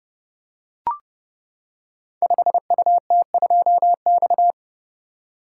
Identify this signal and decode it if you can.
Morse code, heard as a keyed tone, about 30 wpm, 700 Hz. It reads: 5UT2X
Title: Code Groups - 5 Letter:Numbers - 30wpm